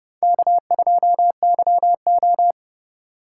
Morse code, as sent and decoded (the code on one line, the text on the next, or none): -.- ..--- -.-- ---
K2YO